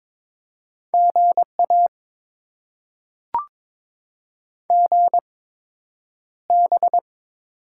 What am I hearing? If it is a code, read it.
Morse code, heard as a keyed tone, about 22 wpm, spaced as for 9 wpm, 700 Hz. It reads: GA GB
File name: Top 90 QSO Elements - 8x Character Spacing - 22wpm